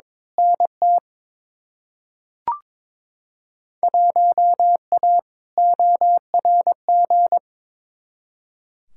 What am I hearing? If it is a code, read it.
Morse code, heard as a keyed tone, about 21 wpm, 700 Hz. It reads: NT 1A ORG
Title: Field Day Exchanges Encore - 22wpm